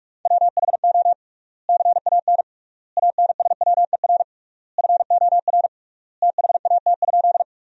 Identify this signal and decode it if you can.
Morse code, heard as a keyed tone, about 46 wpm, 700 Hz. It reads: WHO CAN ANSWER FOR THAT?
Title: 3x Word-Spacing - Sentences from Top 300 Words - 45wpm